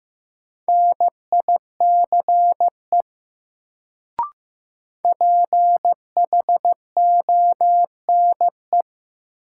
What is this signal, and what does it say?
Morse code, heard as a keyed tone, about 15 wpm, 700 Hz. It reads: NICE PHONE
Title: Sentences from Top 1100 Words - New Word Drill Alt - 15wpm